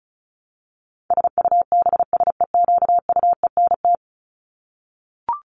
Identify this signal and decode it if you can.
Morse code, heard as a keyed tone, about 34 wpm, 700 Hz. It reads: SUBSEQUENT
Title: LY Suffix - Top 100 Words with Base Words 35wpm